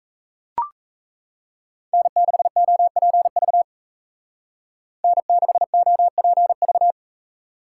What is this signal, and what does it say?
Morse code, heard as a keyed tone, about 34 wpm, 700 Hz. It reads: N6OPV N6OPV
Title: Speed Racing - USA 1x3 Call Signs 28wpm